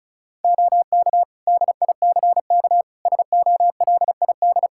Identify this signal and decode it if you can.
Morse code, heard as a keyed tone, about 34 wpm, 700 Hz. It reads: OK DICK SOLID
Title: Practice QSO - Exchange 3 - More Info - 35wpm